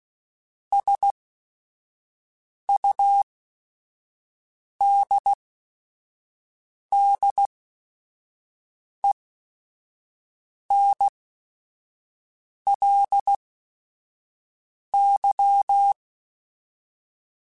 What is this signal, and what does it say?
Morse code, heard as a keyed tone, about 16 wpm, spaced as for 5 wpm, 800 Hz. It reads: SUDDENLY
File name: Technical Subjects / Technical Subjects-mp3s / AND Gate @5